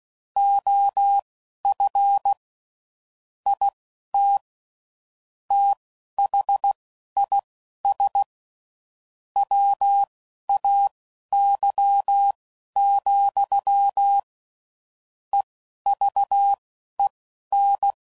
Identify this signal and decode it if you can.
Morse code, heard as a keyed tone, about 16 wpm, spaced as for 11 wpm, 800 Hz. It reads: OF IT THIS WAY, EVEN
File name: CW Code Groups @10WPM / Technical Subjects / OR Gate